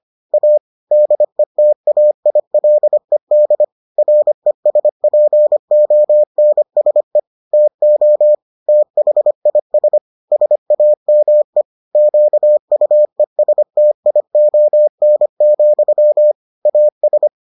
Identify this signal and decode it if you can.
Morse code, heard as a keyed tone, about 25 wpm, 600 Hz. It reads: A DETAILED RESPONSE TO THIS SAME QUESTION, AS